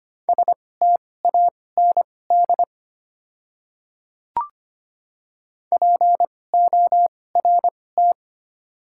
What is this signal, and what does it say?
Morse code, heard as a keyed tone, about 25 wpm, spaced as for 18 wpm, 700 Hz. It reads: STAND PORT
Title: 2x Character Spacing - Top 500 Words - 25wpm